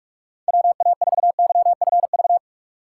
Morse code, heard as a keyed tone, about 44 wpm, 700 Hz. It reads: WA4YFV
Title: USA Call Signs - 6 Characters - 4x Character Spacing - 45wpm